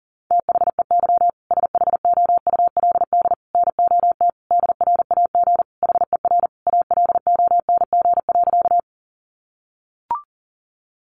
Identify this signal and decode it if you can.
Morse code, heard as a keyed tone, about 40 wpm, 700 Hz. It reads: THEY SHOULD NOT DRAG HER ALONG.
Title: Sentences from Top 1500 Words 40wpm